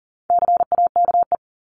Morse code, heard as a keyed tone, about 40 wpm, 700 Hz. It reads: CAKE
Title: Top 1501-1750 Words 40wpm